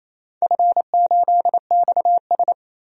Morse code, heard as a keyed tone, about 28 wpm, 700 Hz. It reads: F8XS
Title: DXCC-100 Call Signs - 4 Characters - 28wpm